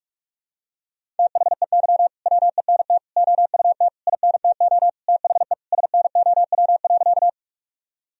Morse code, heard as a keyed tone, about 43 wpm, 700 Hz. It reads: THEY WENT OUT INTO THE SNOW.
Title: Sentences from Top 1000 Words - Review Part 3 - 45wpm